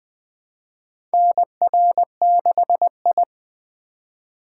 Morse code, heard as a keyed tone, about 21 wpm, 700 Hz. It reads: NR6I
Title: USA Call Signs - 4 Characters - 3x Character Spacing - 20wpm